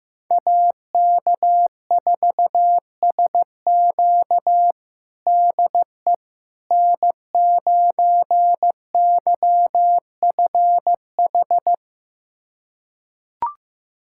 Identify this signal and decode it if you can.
Morse code, heard as a keyed tone, about 16 wpm, 700 Hz. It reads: AK4SQ DE N9YFH
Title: USA Callsign DE Callsign - 5 Characters - 15wpm